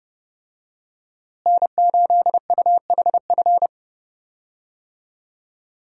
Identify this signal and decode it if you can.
Morse code, heard as a keyed tone, about 32 wpm, 700 Hz. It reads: N8UHF
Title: USA Call Signs - 5 Characters - Repeated 2x - 30wpm